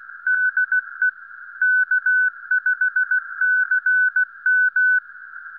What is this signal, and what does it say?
Morse code, heard as a keyed tone, about 16 wpm, 1500 Hz. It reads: DE K5CM
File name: k5cm-80mbyw3jw-snip1